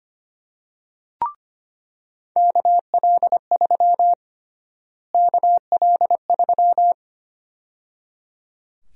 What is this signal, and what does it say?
Morse code, heard as a keyed tone, about 25 wpm, 700 Hz. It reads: KL3 KL3
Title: USA Call Sign Prefixes - Repeated 2x - 25wpm